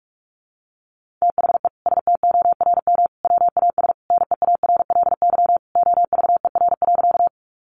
Morse code, heard as a keyed tone, about 46 wpm, 700 Hz. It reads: THE STORM WAS NEARLY OVER.
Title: Sentences from Top 1500 Words 45wpm